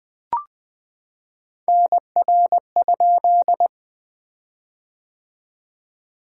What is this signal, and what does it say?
Morse code, heard as a keyed tone, about 21 wpm, 700 Hz. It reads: NR?
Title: Common QSO Phrases - 3x Word-Spacing - 20wpm